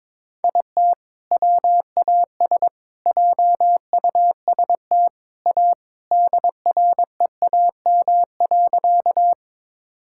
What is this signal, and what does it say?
Morse code, heard as a keyed tone, about 21 wpm, 700 Hz. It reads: IT WAS JUST A DREAM.
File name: practice / 22wpm / Sentences with 3-Word Phrases - 22wpm